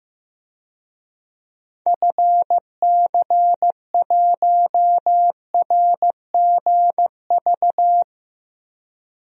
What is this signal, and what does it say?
Morse code, heard as a keyed tone, about 15 wpm, 700 Hz. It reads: FC1RGV
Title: DXCC-100 Call Signs - 6 Characters - 15wpm